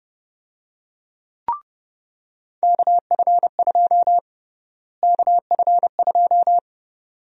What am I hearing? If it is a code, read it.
Morse code, heard as a keyed tone, about 30 wpm, 700 Hz. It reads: KF2 KF2
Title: USA Call Sign Prefixes - Repeated 2x - 30wpm